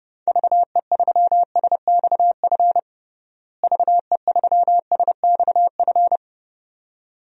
Morse code, heard as a keyed tone, about 31 wpm, 700 Hz. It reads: VE3SXF VE3SXF
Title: DXCC-100 Call Signs - 6 Characters - Repeated 2x - 30wpm